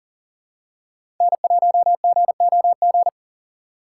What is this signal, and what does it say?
Morse code, heard as a keyed tone, about 40 wpm, 700 Hz. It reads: N1GOG
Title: USA Call Signs - 3x Character Spacing - 40wpm